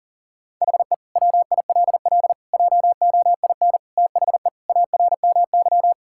Morse code, heard as a keyed tone, about 40 wpm, 690 Hz. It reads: HE WILL JOIN THE ARMY
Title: Sentences from Top 1000 Words - Review Part 6 - 40wpm